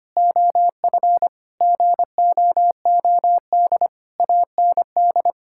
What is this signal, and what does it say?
Morse code, heard as a keyed tone, about 25 wpm, 700 Hz. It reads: OF GOOD AND
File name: practice / 25wpm / Sentences with 9-Word Phrases - 25wpm